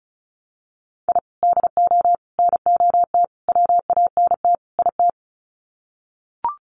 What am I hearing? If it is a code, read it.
Morse code, heard as a keyed tone, about 34 wpm, 700 Hz. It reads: I DO NOT WANT IT
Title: Sets of 5 Words Encore2 35wpm